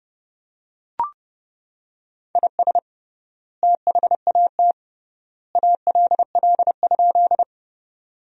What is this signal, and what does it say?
Morse code, heard as a keyed tone, about 30 wpm, 700 Hz. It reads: IS THAT ALL?
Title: 3x Word-Spacing - Sentences from Top 100 Words - 30wpm